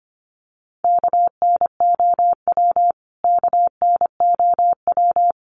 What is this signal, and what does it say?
Morse code, heard as a keyed tone, about 25 wpm, 700 Hz. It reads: KNOW KNOW
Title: Top 500 Words Repeated 25wpm